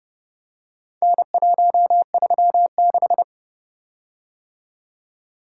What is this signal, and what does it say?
Morse code, heard as a keyed tone, about 29 wpm, 700 Hz. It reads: N136